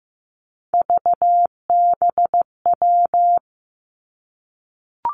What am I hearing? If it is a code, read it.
Morse code, heard as a keyed tone, about 15 wpm, 700 Hz. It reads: VBW
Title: Three Challenging Letter-Number Combinations 15wpm